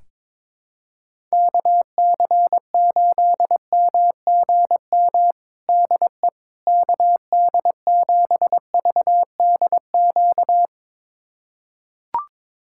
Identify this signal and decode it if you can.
Morse code, heard as a keyed tone, about 22 wpm, 700 Hz. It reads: KC8MGM DE KD7VDQ
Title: USA Callsign DE Callsign - 6 Characters - 22wpm